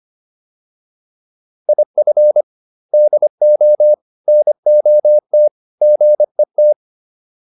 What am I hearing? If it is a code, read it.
Morse code, heard as a keyed tone, about 25 wpm, 600 Hz. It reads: IF DO NOT GET